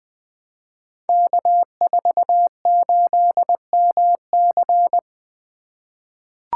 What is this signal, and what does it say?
Morse code, heard as a keyed tone, about 21 wpm, 700 Hz. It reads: K48MC
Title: Code Groups - 5 Letter:Numbers - 20wpm